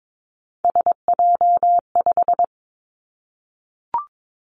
Morse code, heard as a keyed tone, about 22 wpm, 700 Hz. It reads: SJ5
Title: Three Challenging Letter-Number Combinations 22wpm